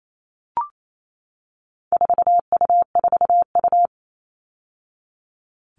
Morse code, heard as a keyed tone, about 28 wpm, 700 Hz. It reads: 4U4U
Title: All Cut Numbers - 4-Digit 28wpm